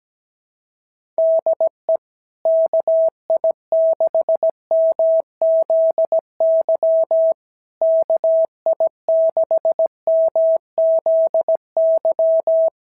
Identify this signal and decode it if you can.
Morse code, heard as a keyed tone, about 15 wpm, 650 Hz. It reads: DE KI6MZY KI6MZY
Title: SOTA QSO Practice 17wpm